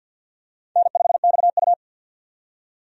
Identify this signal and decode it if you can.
Morse code, heard as a keyed tone, about 51 wpm, 700 Hz. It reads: N5XU
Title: USA Call Signs - 4 Characters - 50wpm